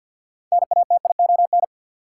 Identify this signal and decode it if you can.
Morse code, heard as a keyed tone, about 49 wpm, 700 Hz. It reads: NATION
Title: Six Letter Words - 50wpm